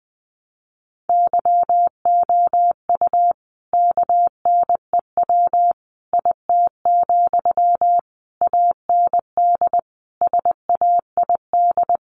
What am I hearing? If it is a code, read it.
Morse code, heard as a keyed tone, about 20 wpm, 700 Hz. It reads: YOU KNEW IT, AND SAID
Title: Sentences from Top 500 Words 20wpm